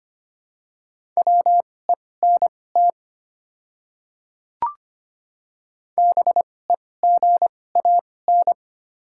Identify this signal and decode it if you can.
Morse code, heard as a keyed tone, about 25 wpm, spaced as for 18 wpm, 700 Hz. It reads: WENT BEGAN
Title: 2x Character Spacing - Top 300 Words - 25wpm